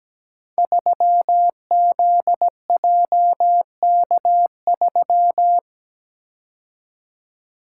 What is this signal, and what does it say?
Morse code, heard as a keyed tone, about 17 wpm, 700 Hz. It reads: 3ZJK3
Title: Code Groups - 5 Letter:Numbers - 17wpm